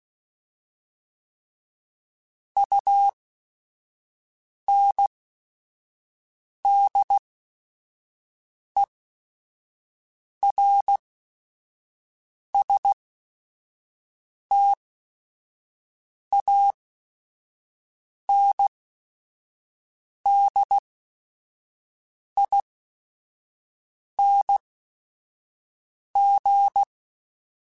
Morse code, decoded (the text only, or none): UNDERSTANDING